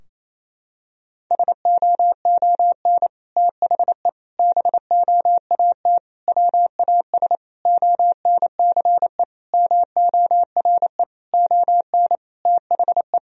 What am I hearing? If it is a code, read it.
Morse code, heard as a keyed tone, about 28 wpm, 700 Hz. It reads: SOON THE BOAT WAS ONCE MORE ON THE
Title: Sentences from Top 1000 Words - Review Part 3 - 28wpm